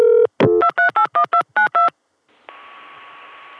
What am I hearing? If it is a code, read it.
DTMF, heard as telephone touch tones, tones of about 100 ms. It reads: 2381292